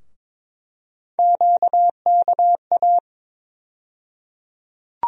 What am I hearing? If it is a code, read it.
Morse code, heard as a keyed tone, about 22 wpm, 700 Hz. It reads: QKA
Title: USA Call Sign Postfixes - 3 Letters 22wpm